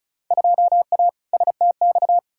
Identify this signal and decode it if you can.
Morse code, heard as a keyed tone, about 36 wpm, 700 Hz. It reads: 2A STX
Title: Field Day Exchanges - 35wpm